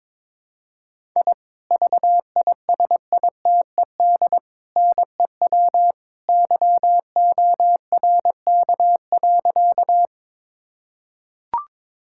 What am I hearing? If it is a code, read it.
Morse code, heard as a keyed tone, about 21 wpm, 700 Hz. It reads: I VISITED NEW YORK.
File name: practice / 22wpm / Sentences with 2-Word Phrases - 22wpm